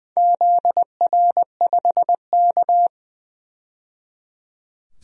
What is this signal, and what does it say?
Morse code, heard as a keyed tone, about 21 wpm, 700 Hz. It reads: ZR5K
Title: DXCC-100 Call Signs - 4 Characters - 20wpm